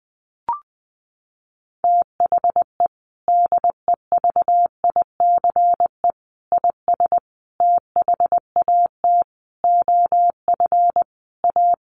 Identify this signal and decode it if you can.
Morse code, heard as a keyed tone, about 19 wpm, 700 Hz. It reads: THE DEVICE IS THAT OF A
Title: Sentences from Top 1500 Words 20wpm